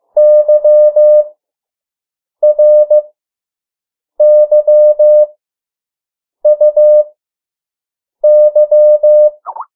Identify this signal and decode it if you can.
Morse code, heard as a keyed tone, about 15 wpm, spaced as for 6 wpm, 600 Hz. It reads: YRYUY
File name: morse mp3 / 03 urok 2